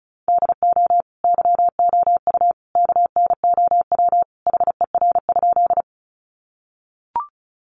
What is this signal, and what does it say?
Morse code, heard as a keyed tone, about 34 wpm, 700 Hz. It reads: DO YOU KNOW HER?
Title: Sentences from Top 100 Words 35wpm